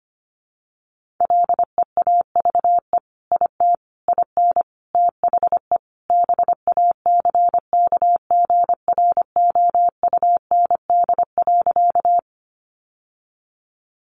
Morse code, heard as a keyed tone, about 25 wpm, 700 Hz. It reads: LEAVE IT IN THE BACKGROUND.
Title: Sentences from Top 1300 Words 25wpm